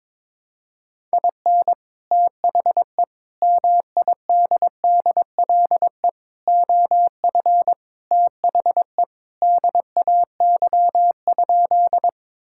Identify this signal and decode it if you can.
Morse code, heard as a keyed tone, about 22 wpm, 700 Hz. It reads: IN THE MIDDLE OF THE DAY?
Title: Sentences from Top 1000 Words - Review Part 6 - 22wpm